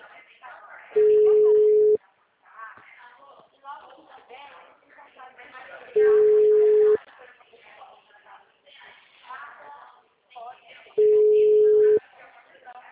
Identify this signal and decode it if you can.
Ringback tone (Europe, 425 Hz), with speech.
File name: out- -2041- - - .22461